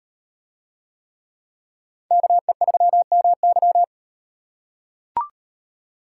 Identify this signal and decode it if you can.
Morse code, heard as a keyed tone, about 38 wpm, 700 Hz. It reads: KE3MY